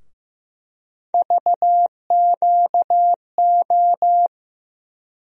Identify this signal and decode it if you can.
Morse code, heard as a keyed tone, about 15 wpm, 700 Hz. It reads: VQO
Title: USA Call Sign Postfixes - 3 Letters 15wpm